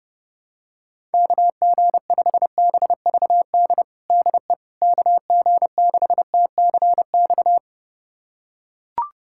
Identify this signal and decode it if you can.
Morse code, heard as a keyed tone, about 30 wpm, 700 Hz. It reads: KG5BVD DE KG6TCX
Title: USA Callsign DE Callsign 30wpm